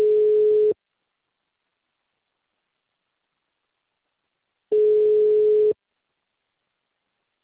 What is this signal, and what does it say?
Ringback tone (Europe, 425 Hz).